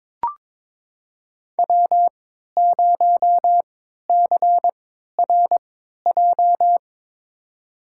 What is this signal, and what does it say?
Morse code, heard as a keyed tone, about 22 wpm, spaced as for 13 wpm, 700 Hz. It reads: W0CRJ
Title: USA Call Signs - 3x Character Spacing - 22wpm